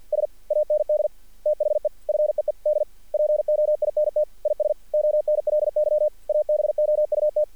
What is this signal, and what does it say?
Morse code, heard as a keyed tone, about 50 wpm, 600 Hz. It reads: S AND THE FEED POINT IS ONLY ABOUT
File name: archived / CRSnet